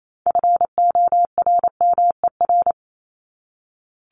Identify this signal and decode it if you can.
Morse code, heard as a keyed tone, about 28 wpm, 700 Hz. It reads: FORMER
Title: LY Suffix - Top 100 Words with Base Words 28wpm